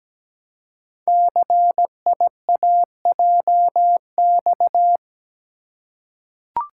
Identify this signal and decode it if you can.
Morse code, heard as a keyed tone, about 17 wpm, 700 Hz. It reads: CIAJX
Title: Code Groups - 5 Letters - 17wpm